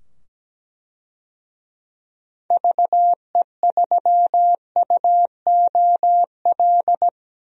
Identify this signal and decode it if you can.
Morse code, heard as a keyed tone, about 17 wpm, 700 Hz. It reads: VE3UOL
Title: DXCC-100 Call Signs - 6 Characters - 17wpm